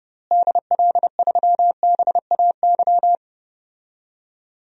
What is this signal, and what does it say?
Morse code, heard as a keyed tone, about 30 wpm, 700 Hz. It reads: DL3BAY
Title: DXCC-100 Call Signs - 6 Characters - Repeated 3x - 30wpm